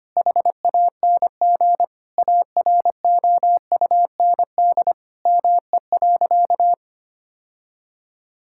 Morse code, heard as a keyed tone, about 25 wpm, 700 Hz. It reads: HANG AROUND ME.
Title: Sentences from Top 1600 Words 25wpm